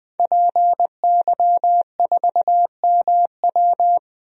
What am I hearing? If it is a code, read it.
Morse code, heard as a keyed tone, about 20 wpm, 700 Hz. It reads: PY4MW
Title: DXCC-100 Call Signs - 5 Characters - Repeated 3x - 20wpm